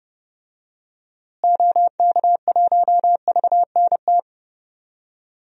Morse code, heard as a keyed tone, about 30 wpm, 700 Hz. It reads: OK1VNT
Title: DXCC-100 Call Signs - 6 Characters - Repeated 2x - 30wpm